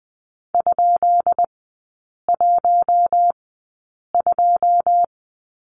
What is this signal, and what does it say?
Morse code, heard as a keyed tone, about 20 wpm, spaced as for 8 wpm, 700 Hz. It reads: ?12
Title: 198 Lesson 32e - Letter k - Three Characters 20wpm